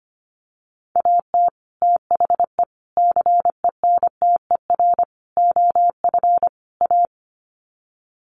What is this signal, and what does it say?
Morse code, heard as a keyed tone, about 25 wpm, 700 Hz. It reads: AT THE CENTER OF A
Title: Sets of 5 Words Encore2 25wpm